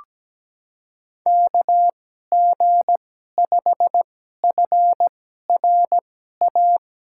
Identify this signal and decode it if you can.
Morse code, heard as a keyed tone, about 17 wpm, spaced as for 12 wpm, 700 Hz. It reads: KG5FRA
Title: USA Call Signs - 2x Character Spacing - 17wpm